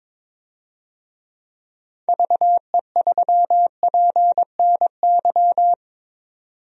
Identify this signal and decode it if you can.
Morse code, heard as a keyed tone, about 22 wpm, 700 Hz. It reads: VE3PNY
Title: DXCC-100 Call Signs - 6 Characters - 22wpm